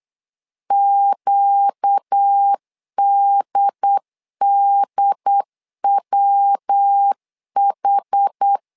Morse code, heard as a keyed tone, about 8 wpm, 800 Hz. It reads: QDDWH
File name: LEZIONI / 4 Q 8wpm 2732